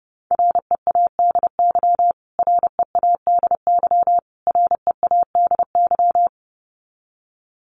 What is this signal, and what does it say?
Morse code, heard as a keyed tone, about 30 wpm, 700 Hz. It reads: READY READY READY
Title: Top 500 Words Repeated 30wpm